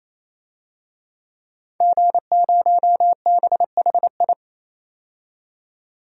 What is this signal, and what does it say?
Morse code, heard as a keyed tone, about 28 wpm, 700 Hz. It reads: G0BHI
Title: DXCC-100 Call Signs - 5 Characters - 28wpm